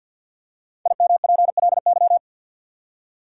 Morse code, heard as a keyed tone, about 46 wpm, 700 Hz. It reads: IMPLY
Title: Five Letter Words - 50wpm